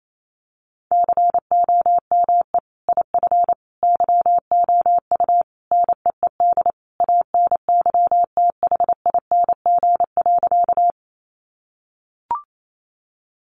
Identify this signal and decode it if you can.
Morse code, heard as a keyed tone, about 28 wpm, 700 Hz. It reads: COME IF YOU NEED ANYTHING.
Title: Sentences from Top 1000 Words 28wpm